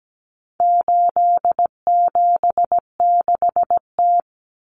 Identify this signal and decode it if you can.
Morse code, heard as a keyed tone, about 17 wpm, 700 Hz. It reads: 876T
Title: Common Cut Numbers - 4-Digit 17wpm